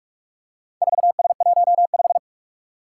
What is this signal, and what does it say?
Morse code, heard as a keyed tone, about 44 wpm, 700 Hz. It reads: 4S15
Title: Four Challenging Letter-Number Combinations 45wpm